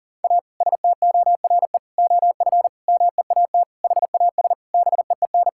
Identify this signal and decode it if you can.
Morse code, heard as a keyed tone, about 40 wpm, 700 Hz. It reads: A STORE OF MEAT HAS BEEN
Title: Sentences from Top 1000 Words - Review Part 5 - 40wpm